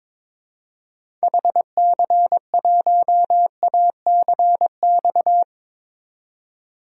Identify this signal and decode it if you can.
Morse code, heard as a keyed tone, about 22 wpm, 700 Hz. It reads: HC1ACX
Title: DXCC-100 Call Signs - 6 Characters - 22wpm